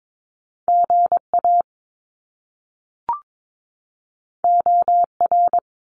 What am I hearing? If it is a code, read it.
Morse code, heard as a keyed tone, about 22 wpm, 700 Hz. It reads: GA OR
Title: Top Two Letter Combinations in Words 22wpm